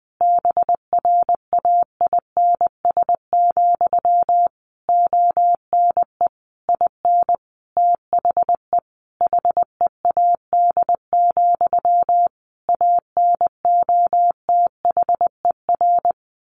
Morse code, heard as a keyed tone, about 20 wpm, 700 Hz. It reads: BRAINS, ONE IN THE HEAD, ANOTHER
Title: A Sound Of Thunder 20wpm